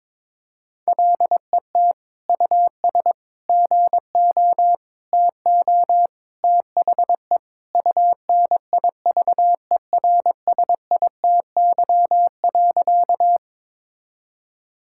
Morse code, heard as a keyed tone, about 21 wpm, 700 Hz. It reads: LET US GO TO THE UNIVERSITY.
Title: Sentences from Top 1000 Words - Review Part 5 - 22wpm